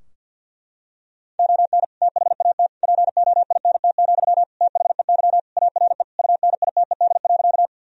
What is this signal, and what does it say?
Morse code, heard as a keyed tone, about 46 wpm, 700 Hz. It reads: ON THAT POINT, THEY ARE UNITED.
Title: Sentences from Top 1000 Words - Review Part 4 - 50wpm